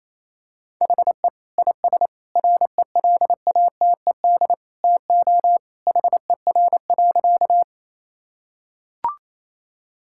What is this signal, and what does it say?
Morse code, heard as a keyed tone, about 28 wpm, 700 Hz. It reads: HE IS RELATED TO HER.
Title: Sentences from Top 1000 Words - Review Part 5 - 28wpm